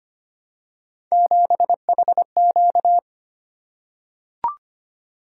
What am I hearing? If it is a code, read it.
Morse code, heard as a keyed tone, about 25 wpm, 700 Hz. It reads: 7HQ
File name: Three Letter-Number Combinations - 25wpm